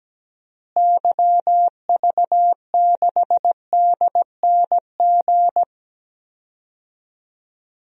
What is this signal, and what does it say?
Morse code, heard as a keyed tone, about 17 wpm, 700 Hz. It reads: YV6DNG
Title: DXCC-100 Call Signs - 6 Characters - Repeated 2x - 17wpm